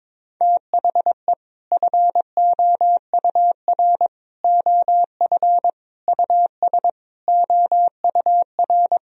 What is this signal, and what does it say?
Morse code, heard as a keyed tone, about 21 wpm, 700 Hz. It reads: THE FOUR OF US OUR
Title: Sentences from Top 1000 Words - Review Part 1 - 22wpm